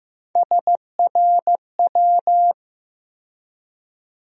Morse code, heard as a keyed tone, about 15 wpm, 700 Hz. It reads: SRW
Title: USA Call Sign Postfixes - 3 Letters 15wpm